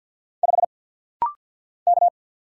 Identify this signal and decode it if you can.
Morse code, heard as a keyed tone, about 50 wpm, 700 Hz. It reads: 5 K
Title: Single Letter-Number - Rapid-Fire 50wpm